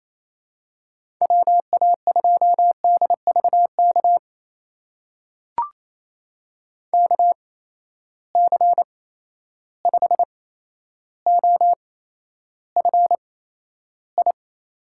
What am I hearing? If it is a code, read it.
Morse code, heard as a keyed tone, about 28 wpm, spaced as for 12 wpm, 700 Hz. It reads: WA2DVK KC5OFI
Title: USA Call Signs - 6 Characters - 8x Character Spacing - 28wpm